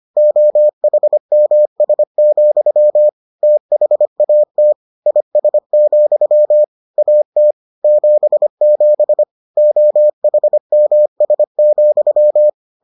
Morse code, heard as a keyed tone, about 25 wpm, 600 Hz. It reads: OHMS, THAT IS, AT 77 OHMS,